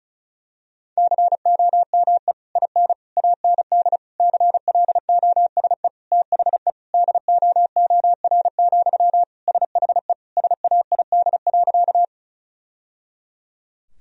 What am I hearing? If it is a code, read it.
Morse code, heard as a keyed tone, about 36 wpm, 700 Hz. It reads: COME IN AND CLOSE THE DOOR, SHE SAID.
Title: Sentences from Top 1000 Words - Review Part 1 - 35wpm